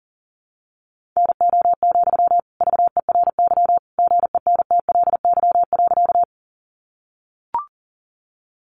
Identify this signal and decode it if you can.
Morse code, heard as a keyed tone, about 40 wpm, 700 Hz. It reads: NO, VERY GENTLY.
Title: Sentences from Top 1500 Words 40wpm